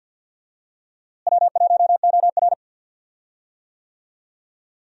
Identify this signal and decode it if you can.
Morse code, heard as a keyed tone, about 52 wpm, 700 Hz. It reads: W1OR
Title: USA Call Signs - 4 Characters - Repeated 3x - 50wpm